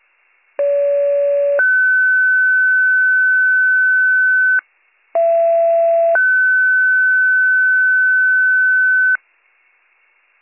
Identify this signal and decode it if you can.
Two-tone page, heard as a one-second tone, then a three-second tone, about 570 then 1500 Hz, about 670 then 1500 Hz. It is heard from 0.6 to 9.2 s.